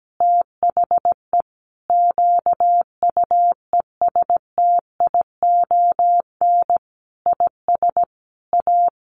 Morse code, heard as a keyed tone, about 17 wpm, 700 Hz. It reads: THE QUESTION IS A
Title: Sentences from Top 1200 Words 17wpm